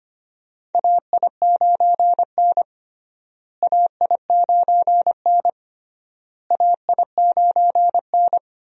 Morse code, heard as a keyed tone, about 25 wpm, 700 Hz. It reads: AI9N AI9N AI9N
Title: USA Call Signs - 4 Characters - Repeated 3x - 25wpm